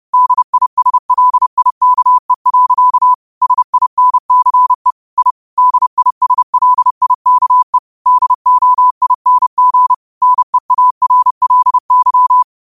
Morse code, heard as a keyed tone, about 30 wpm, 1000 Hz. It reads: DISLIKE. SINCE I DISLIKE DOING NEARLY
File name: qotdfiles / 30wpm